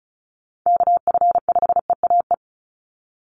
Morse code, heard as a keyed tone, about 34 wpm, 700 Hz. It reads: KF5EAE
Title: USA Call Signs 35wpm